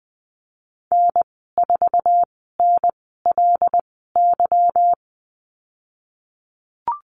Morse code, heard as a keyed tone, about 20 wpm, spaced as for 14 wpm, 700 Hz. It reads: N4NLY
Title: 167 Lesson 27g - Number 7 - Call Signs 20wpm